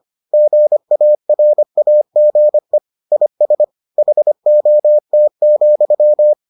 Morse code, heard as a keyed tone, about 25 wpm, 600 Hz. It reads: GARAGE IS HOT,